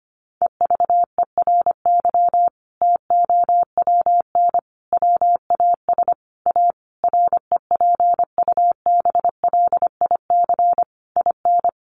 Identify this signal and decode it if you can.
Morse code, heard as a keyed tone, about 25 wpm, 700 Hz. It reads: EVERY TOWN WAS A REPUBLIC IN
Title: Sentences from Top 1500 Words 25wpm